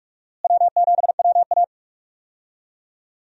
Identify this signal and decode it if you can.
Morse code, heard as a keyed tone, about 46 wpm, 700 Hz. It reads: W7WA
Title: USA Call Signs - 4 Characters - Repeated 3x - 45wpm